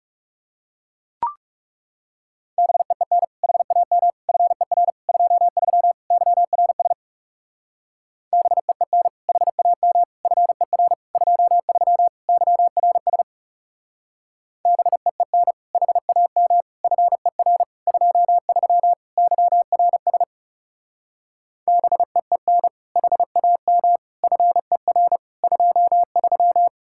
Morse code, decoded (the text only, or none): BEEN HAM FER 23 YRS BEEN HAM FER 23 YRS BEEN HAM FER 23 YRS BEEN HAM FER 23